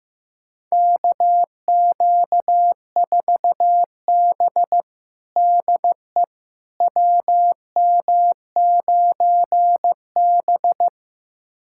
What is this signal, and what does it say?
Morse code, heard as a keyed tone, about 16 wpm, 700 Hz. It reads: KQ4B DE WM9B
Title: USA Callsign DE Callsign - 4 Characters - 15wpm